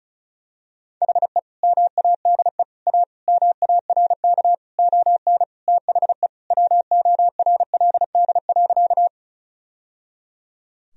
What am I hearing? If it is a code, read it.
Morse code, heard as a keyed tone, about 36 wpm, 700 Hz. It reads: HE MADE A MARK ON THE WORLD.
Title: Sentences from Top 1000 Words - Review Part 1 - 35wpm